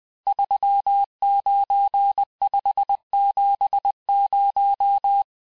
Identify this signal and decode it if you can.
Morse code, heard as a keyed tone, about 20 wpm, 800 Hz. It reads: 39570
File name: cours-858-texte-chiffre-vitesse-19mpm